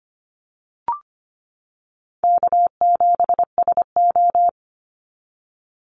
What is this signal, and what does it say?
Morse code, heard as a keyed tone, about 25 wpm, 700 Hz. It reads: K7SO
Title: NA SOTA Activator Callsigns 25wpm